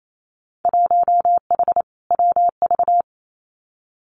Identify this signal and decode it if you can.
Morse code, heard as a keyed tone, about 28 wpm, 700 Hz. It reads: 1H WV
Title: Winter Field Day Exchanges 28wpm